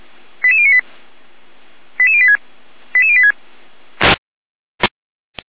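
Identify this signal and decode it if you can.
Five-tone selective call (ZVEI), heard as quick runs of five tones, tones of about 80 ms. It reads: 88098 88086 88086
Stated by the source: ZVEI five-tone selective call 88052 received via SDR.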